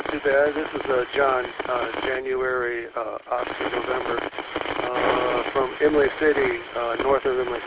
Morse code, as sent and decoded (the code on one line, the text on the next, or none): none